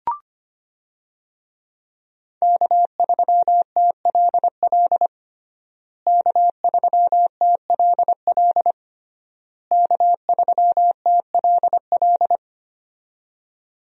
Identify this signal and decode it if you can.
Morse code, heard as a keyed tone, about 25 wpm, 700 Hz. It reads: K3TLL K3TLL K3TLL
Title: USA Call Signs - 5 Characters - Repeated 3x - 25wpm